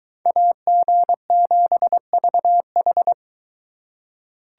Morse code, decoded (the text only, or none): AG7VH